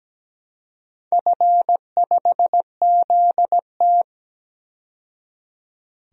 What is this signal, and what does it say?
Morse code, heard as a keyed tone, about 17 wpm, 700 Hz. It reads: F5ZT
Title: DXCC-100 Call Signs - 4 Characters - 17wpm